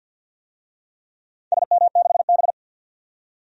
Morse code, heard as a keyed tone, about 52 wpm, 700 Hz. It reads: SM6B
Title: DXCC-100 Call Signs - 4 Characters - 50wpm